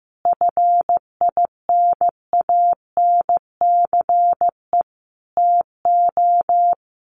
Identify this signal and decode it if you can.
Morse code, heard as a keyed tone, about 14 wpm, 700 Hz. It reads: FINANCE TO
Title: Sentences from Top 1500 Words 15wpm